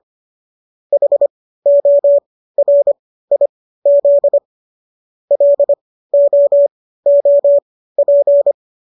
Morse code, heard as a keyed tone, about 25 wpm, spaced as for 15 wpm, 570 Hz. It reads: HORIZ LOOP